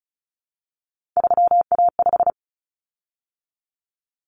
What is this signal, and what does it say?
Morse code, heard as a keyed tone, about 34 wpm, 700 Hz. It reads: 3A5
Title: Common Cut Numbers - 3-Digit 35wpm